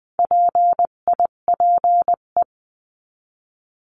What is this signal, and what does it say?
Morse code, heard as a keyed tone, about 20 wpm, 700 Hz. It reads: PIPE
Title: 258 Lesson 42c - Words 16-20wpm